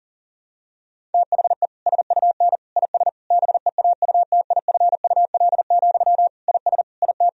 Morse code, heard as a keyed tone, about 40 wpm, 700 Hz. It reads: THE SUN IS BEAUTIFUL, IS IT
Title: Sentences from Top 1000 Words - Review Part 1 - 40wpm